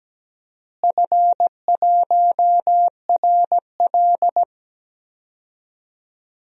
Morse code, heard as a keyed tone, about 17 wpm, 700 Hz. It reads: F1RL
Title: DXCC-100 Call Signs - 4 Characters - Repeated 2x - 17wpm